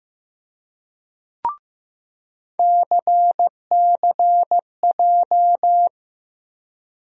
Morse code, heard as a keyed tone, about 15 wpm, 700 Hz. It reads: CCJ
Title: USA Call Sign Postfixes - 3 Letters 15wpm